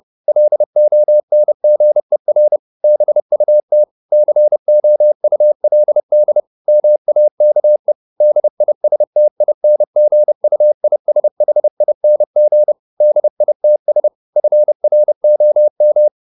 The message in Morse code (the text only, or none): LONGER BUT COULD MAKE DISTINGUISHING DITS FROM